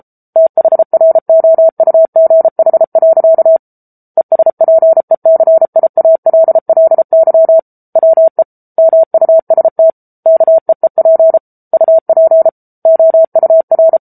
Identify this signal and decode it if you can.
Morse code, heard as a keyed tone, about 34 wpm, 650 Hz. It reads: THROUGH. ESPECIALLY WE MUST KEEP UP OUR
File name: Book Two / 07-Chapter 7 - 33 WPM